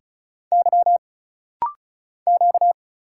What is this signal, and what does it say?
Morse code, heard as a keyed tone, about 35 wpm, 700 Hz. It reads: Y Q
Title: Single Letters - Rapid-Fire 35wpm